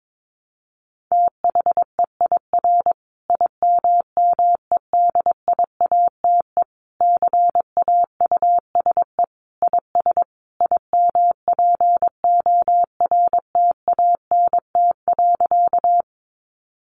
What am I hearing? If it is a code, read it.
Morse code, heard as a keyed tone, about 22 wpm, 700 Hz. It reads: THEIR IMMEDIATE CAUSE IS IMPORTANT.
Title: Sentences from Top 1200 Words 22wpm